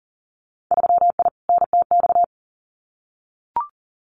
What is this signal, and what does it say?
Morse code, heard as a keyed tone, about 40 wpm, 700 Hz. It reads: 3I NTX
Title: Winter Field Day Exchanges 40wpm